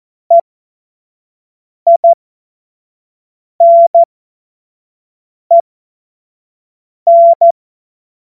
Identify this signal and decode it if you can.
Morse code, heard as a keyed tone, about 14 wpm, spaced as for 5 wpm, 680 Hz. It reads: EINEN
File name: Morsekurs / Witze / Witz 25. Oktober 2022 25 F70